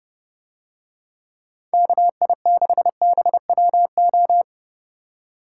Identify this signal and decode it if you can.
Morse code, heard as a keyed tone, about 30 wpm, 700 Hz. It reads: KI6BWO